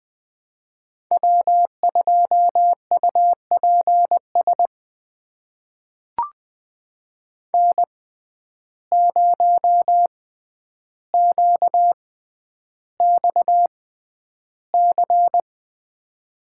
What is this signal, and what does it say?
Morse code, heard as a keyed tone, about 20 wpm, spaced as for 10 wpm, 700 Hz. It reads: W2UPS N0QXC
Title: USA Call Signs - 5 Characters - 6x Character Spacing - 20wpm